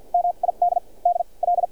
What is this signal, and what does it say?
Morse code, heard as a keyed tone, about 51 wpm, spaced as for 32 wpm, 700 Hz. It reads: MIDDL
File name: archived / CRSnet